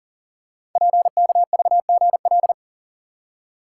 Morse code, heard as a keyed tone, about 38 wpm, 700 Hz. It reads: PKVGL
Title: Code Groups - 5 Letters - 40wpm